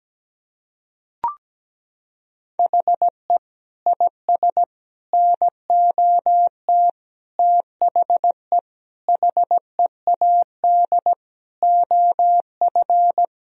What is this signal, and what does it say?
Morse code, heard as a keyed tone, about 17 wpm, 700 Hz. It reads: HE IS NOT THE HEAD OF
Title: Sentences from Top 1000 Words - Review Part 2 - 17wpm